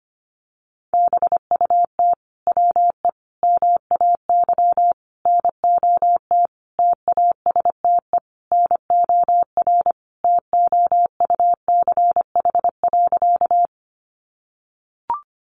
Morse code, heard as a keyed tone, about 25 wpm, 700 Hz. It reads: BUT WE MAY NOT TASTE NOR TOUCH.
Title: Sentences from Top 1500 Words 25wpm